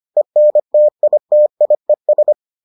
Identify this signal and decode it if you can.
Morse code, heard as a keyed tone, about 25 wpm, 600 Hz. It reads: ENTITIES